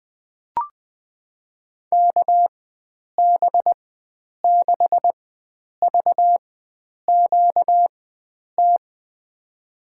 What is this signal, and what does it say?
Morse code, heard as a keyed tone, about 20 wpm, spaced as for 9 wpm, 700 Hz. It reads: KB6VQT
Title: USA Call Signs - 6 Characters - 4x Character Spacing - 20wpm